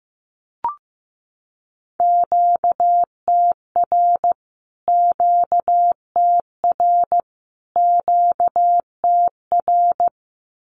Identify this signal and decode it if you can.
Morse code, heard as a keyed tone, about 15 wpm, 700 Hz. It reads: QTR QTR QTR
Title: Q-Codes Repeated 3x 15wpm